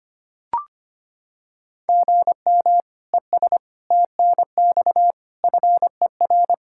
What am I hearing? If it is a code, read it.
Morse code, heard as a keyed tone, about 25 wpm, 700 Hz. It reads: GM ES TNX FER
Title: Practice QSO - Exchange 1 - Beginning - 25wpm